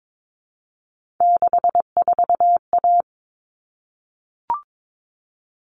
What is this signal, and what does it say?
Morse code, heard as a keyed tone, about 22 wpm, 700 Hz. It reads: 64A
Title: All Cut Numbers - 3-Digit 22wpm